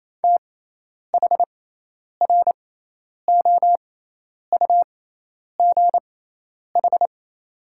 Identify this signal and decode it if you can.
Morse code, heard as a keyed tone, about 28 wpm, spaced as for 10 wpm, 700 Hz. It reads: THROUGH
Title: 6x Character Spacing - Top 300 Words - 28wpm